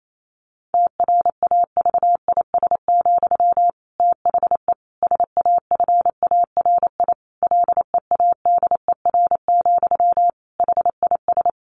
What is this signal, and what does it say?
Morse code, heard as a keyed tone, about 28 wpm, 700 Hz. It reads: TRAVIS, THE SAFARI LEADER, HIS
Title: A Sound Of Thunder 28wpm